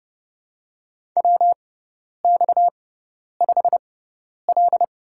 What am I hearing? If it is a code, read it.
Morse code, heard as a keyed tone, about 30 wpm, spaced as for 10 wpm, 700 Hz. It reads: WX5L
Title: USA Call Signs - 4 Characters - 6x Character Spacing - 30wpm